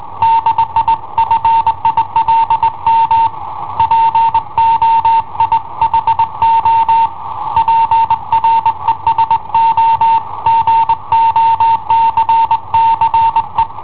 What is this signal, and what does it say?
Morse code, heard as a keyed tone, about 21 wpm, 910 Hz. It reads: DI FILM POI HO PRESO GOCCE
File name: i7ohp / cassetta i7ohp B